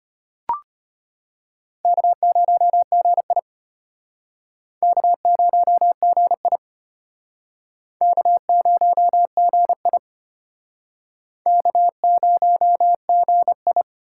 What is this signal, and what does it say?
Morse code, heard as a keyed tone, about 34 wpm, 700 Hz. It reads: K0GI K0GI K0GI K0GI